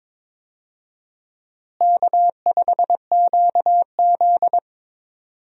Morse code, heard as a keyed tone, about 22 wpm, 700 Hz. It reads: K5QZ
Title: USA Call Signs - 4 Characters - Repeated 2x - 22wpm